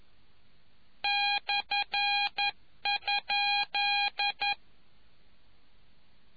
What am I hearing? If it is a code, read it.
Morse code, heard as a keyed tone, about 11 wpm, 810 Hz. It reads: /?